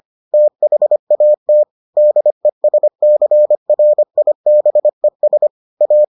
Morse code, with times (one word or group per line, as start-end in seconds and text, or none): 0.3-1.6 THAT
2.0-5.5 DESCRIBES
5.8-6.0 A